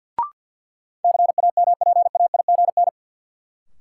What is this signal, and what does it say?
Morse code, heard as a keyed tone, about 53 wpm, 700 Hz. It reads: CAMPAIGN
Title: Sentences from Top 1200 Words - New Word Drill Alt - 50wpm